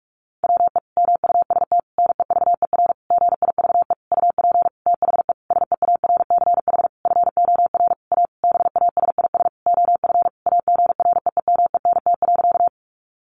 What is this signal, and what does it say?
Morse code, heard as a keyed tone, about 44 wpm, 700 Hz. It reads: WE MUST NEVER GIVE UP THE SEARCH FOR A BASIS OF AGREEMENT.
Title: Sentences from Top 1500 Words 45wpm